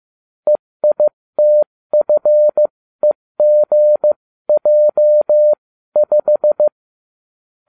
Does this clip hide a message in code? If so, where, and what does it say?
Yes: Morse code from 0.5 to 6.7 s, heard as a keyed tone, about 15 wpm, spaced as for 13 wpm, 600 Hz. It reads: EITFEGJ5